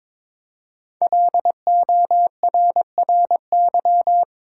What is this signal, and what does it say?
Morse code, heard as a keyed tone, about 22 wpm, 700 Hz. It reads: LORRY
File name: 5 Letter Words - First 3 Letters Given - 22wpm